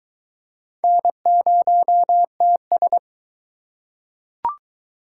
Morse code, heard as a keyed tone, about 22 wpm, 700 Hz. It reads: N0TS